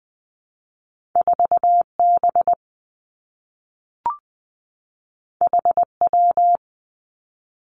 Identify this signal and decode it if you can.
Morse code, heard as a keyed tone, about 20 wpm, 700 Hz. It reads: 4 B H W
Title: Two Challenging Letter-Number Combinations 20wpm